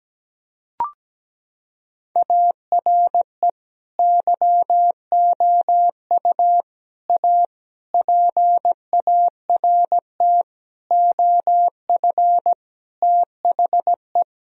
Morse code, heard as a keyed tone, about 17 wpm, 700 Hz. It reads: ARE YOU A PART OF THE
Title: Sentences from Top 1000 Words - Review Part 4 - 17wpm